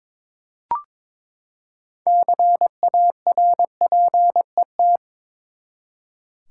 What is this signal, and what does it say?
Morse code, heard as a keyed tone, about 21 wpm, 700 Hz. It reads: CARPET
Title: Six Letter Words - 22wpm